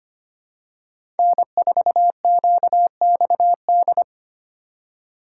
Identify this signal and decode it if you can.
Morse code, heard as a keyed tone, about 25 wpm, 700 Hz. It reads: N4QXD
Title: USA Call Signs - 5 Characters - 4x Character Spacing - 25wpm